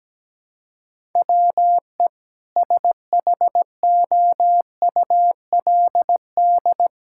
Morse code, heard as a keyed tone, about 17 wpm, 700 Hz. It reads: WE SHOULD
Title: Sentences from Top 1000 Words - Review Part 5 - 17wpm